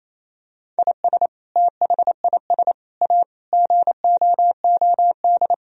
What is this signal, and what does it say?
Morse code, heard as a keyed tone, about 28 wpm, 700 Hz. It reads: IS THIS A GOOD
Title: Sentences from Top 1000 Words - Review Part 3 - 28wpm